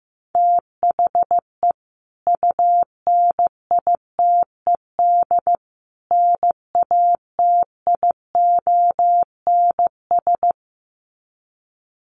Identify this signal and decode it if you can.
Morse code, heard as a keyed tone, about 15 wpm, 700 Hz. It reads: THE UNITED NATIONS
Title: Sets of 3 Words 15wpm